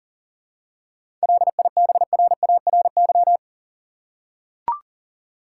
Morse code, heard as a keyed tone, about 40 wpm, 700 Hz. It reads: LIBRARY
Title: Seven Letter Words - 40wpm